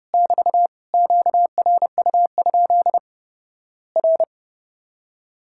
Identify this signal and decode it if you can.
Morse code, heard as a keyed tone, about 30 wpm, 700 Hz. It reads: <BT> QRU? R
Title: Worked All Europe DX Contest - QTC Exchange - 30wpm